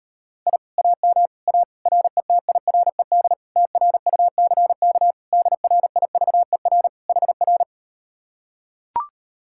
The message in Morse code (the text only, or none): I AM A RETIRED TRUCK DRIVER HR